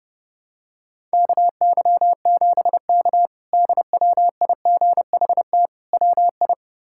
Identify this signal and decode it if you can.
Morse code, heard as a keyed tone, about 30 wpm, 700 Hz. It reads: KY7K DWIGHT WI
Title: K1USN SST QSO Practice - 30wpm